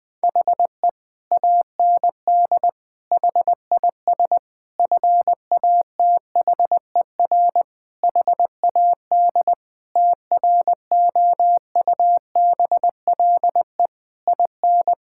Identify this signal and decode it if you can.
Morse code, heard as a keyed tone, about 20 wpm, 700 Hz. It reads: HE AND HIS FATHER HAD TROUBLE IN
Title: Sentences from Top 1600 Words 20wpm